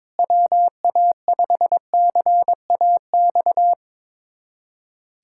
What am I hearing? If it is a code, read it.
Morse code, heard as a keyed tone, about 22 wpm, 700 Hz. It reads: WA5CAX
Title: USA Callsign DE Callsign - 6 Characters - 22wpm